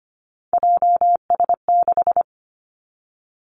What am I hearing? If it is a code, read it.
Morse code, heard as a keyed tone, about 25 wpm, 700 Hz. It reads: JS6
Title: Three Challenging Letter-Number Combinations 25wpm